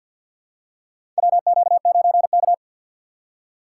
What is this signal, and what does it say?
Morse code, heard as a keyed tone, about 50 wpm, 700 Hz. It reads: WQ9K
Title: USA Call Signs - 4 Characters - 3x Character Spacing - 50wpm